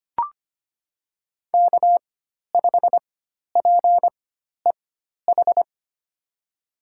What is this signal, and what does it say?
Morse code, heard as a keyed tone, about 25 wpm, spaced as for 12 wpm, 700 Hz. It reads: K5PEH
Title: USA Call Signs - 4x Character Spacing - 25wpm